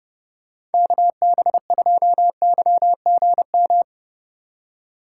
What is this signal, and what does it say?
Morse code, heard as a keyed tone, about 31 wpm, 700 Hz. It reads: KB2YGM
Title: USA Call Signs - 3x Character Spacing - 30wpm